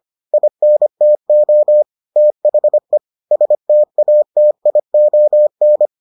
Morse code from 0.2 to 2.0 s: INTO

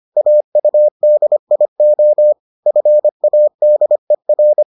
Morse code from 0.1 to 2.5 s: AUDIO

